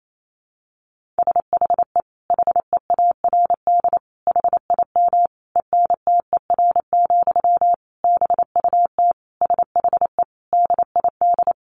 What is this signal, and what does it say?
Morse code, heard as a keyed tone, about 28 wpm, 700 Hz. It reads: SHE HEARD HIM ENTER, BUT SHE DID
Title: Sentences from Top 1400 Words 28wpm